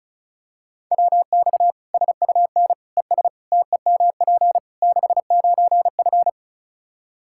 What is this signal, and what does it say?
Morse code, heard as a keyed tone, about 36 wpm, 700 Hz. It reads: WX SUN ES TEMP 69F
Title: K1USN SST QSO Practice 35wpm